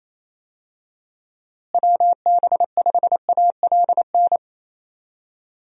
Morse code, heard as a keyed tone, about 28 wpm, 700 Hz. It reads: WB5ALN
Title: USA Call Signs - 6 Characters - Repeated 2x - 28wpm